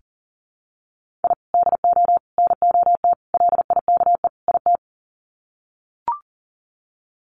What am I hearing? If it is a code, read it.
Morse code, heard as a keyed tone, about 40 wpm, 700 Hz. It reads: I DO NOT LIKE IT
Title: Sets of 5 Words Encore 40wpm